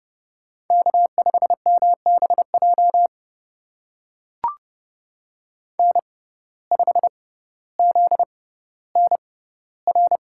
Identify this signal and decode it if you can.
Morse code, heard as a keyed tone, about 30 wpm, spaced as for 15 wpm, 700 Hz. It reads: K5MBJ N5ZNR
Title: USA Call Signs - 5 Characters - 6x Character Spacing - 30wpm